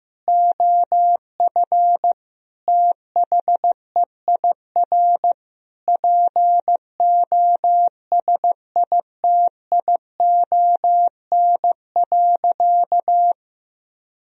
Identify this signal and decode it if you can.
Morse code, heard as a keyed tone, about 16 wpm, 700 Hz. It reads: OF THEIR POSITION.
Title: Sentences from Top 1000 Words - Review Part 3 - 15wpm